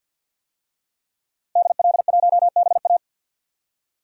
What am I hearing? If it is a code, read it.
Morse code, heard as a keyed tone, about 52 wpm, 700 Hz. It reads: DL1BA
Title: DXCC-100 Call Signs - 5 Characters - Repeated 2x - 50wpm